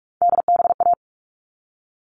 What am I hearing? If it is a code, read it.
Morse code, heard as a keyed tone, about 44 wpm, 700 Hz. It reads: DBA
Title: All Cut Numbers - 3-Digit 45wpm